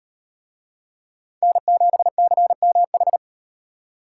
Morse code, heard as a keyed tone, about 38 wpm, 700 Hz. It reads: N7CMH